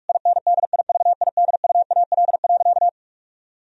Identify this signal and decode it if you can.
Morse code, heard as a keyed tone, about 46 wpm, 700 Hz. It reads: INDIVIDUAL.
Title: Sentences from Top 1000 Words - Review Part 4 - 45wpm